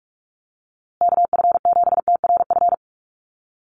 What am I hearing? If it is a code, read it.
Morse code, heard as a keyed tone, about 44 wpm, 700 Hz. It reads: KF7TRF
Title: USA Call Signs Repeated 2x 45wpm